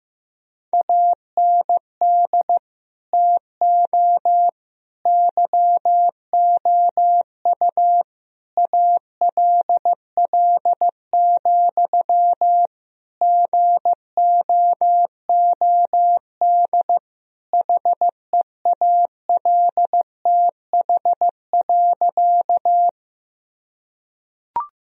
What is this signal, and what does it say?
Morse code, heard as a keyed tone, about 15 wpm, 700 Hz. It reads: AND TO YOU ALL, GOOD HEALTH.
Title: Sentences from Top 1000 Words - Review Part 3 - 15wpm